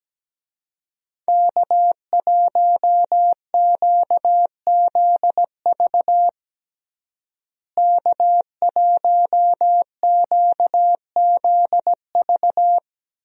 Morse code, heard as a keyed tone, about 17 wpm, 700 Hz. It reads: K1QZV K1QZV
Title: USA Call Signs - 5 Characters - Repeated 2x - 17wpm